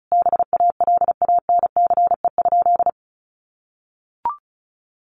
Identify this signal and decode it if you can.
Morse code, heard as a keyed tone, about 34 wpm, 700 Hz. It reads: BALANCE?
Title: Sentences from Top 900 Words 35wpm